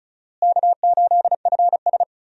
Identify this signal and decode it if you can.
Morse code, heard as a keyed tone, about 36 wpm, 700 Hz. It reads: K8FS
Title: USA Call Signs - 4 Characters - Repeated 3x - 35wpm